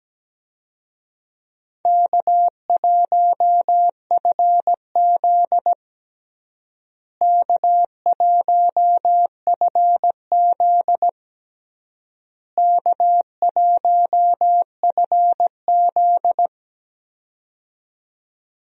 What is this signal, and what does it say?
Morse code, heard as a keyed tone, about 17 wpm, 700 Hz. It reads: K1FZ K1FZ K1FZ
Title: USA Call Signs - 4 Characters - Repeated 3x - 17wpm